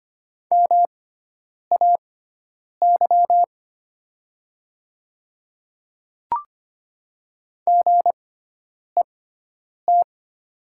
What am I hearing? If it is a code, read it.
Morse code, heard as a keyed tone, about 25 wpm, spaced as for 9 wpm, 700 Hz. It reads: MAY GET
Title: 6x Character Spacing - Top 300 Words - 25wpm